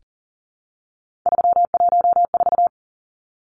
Morse code, heard as a keyed tone, about 40 wpm, 700 Hz. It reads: 314
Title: Numbers - 3-Digit 40wpm